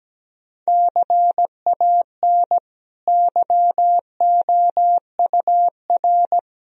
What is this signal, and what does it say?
Morse code, heard as a keyed tone, about 17 wpm, 700 Hz. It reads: CAN YOUR
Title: Sentences from Top 1000 Words - Review Part 1 - 17wpm